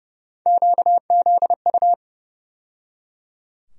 Morse code, heard as a keyed tone, about 30 wpm, 700 Hz. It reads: QZU
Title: USA Call Sign Postfixes - 3 Letters 30wpm